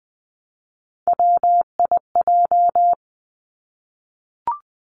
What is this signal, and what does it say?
Morse code, heard as a keyed tone, about 20 wpm, 700 Hz. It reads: WIJ
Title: Three Challenging Letter-Number Combinations 20wpm